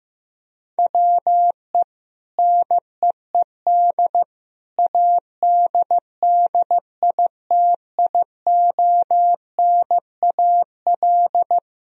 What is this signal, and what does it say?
Morse code, heard as a keyed tone, about 15 wpm, 700 Hz. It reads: WE NEED ADDITIONAL
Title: Sentences from Top 1000 Words - Review Part 4 - 15wpm